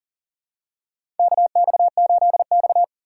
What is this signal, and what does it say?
Morse code, heard as a keyed tone, about 40 wpm, 700 Hz. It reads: KX8X
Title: K1USN SST QSO Practice - 40wpm